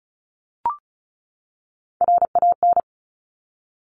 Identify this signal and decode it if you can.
Morse code, heard as a keyed tone, about 36 wpm, 700 Hz. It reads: RAN
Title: Top Three Letter Combinations in Words 35wpm